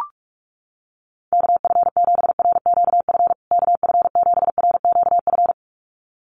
Morse code, heard as a keyed tone, about 44 wpm, 700 Hz. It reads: KF7RQF KF7RQF
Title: USA Call Signs Repeated 2x 45wpm